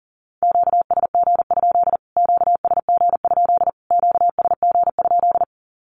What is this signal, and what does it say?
Morse code, heard as a keyed tone, about 40 wpm, 700 Hz. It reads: QSG? QSG? QSG?
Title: Q-Codes Repeated 3x 40wpm